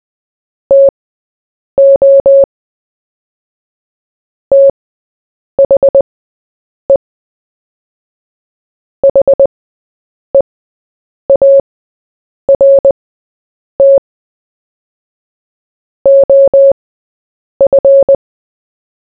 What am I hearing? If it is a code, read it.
Morse code, heard as a keyed tone, about 20 wpm, spaced as for 8 wpm, 550 Hz. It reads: TO THE HEART OF
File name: Short-Story-4-8wpm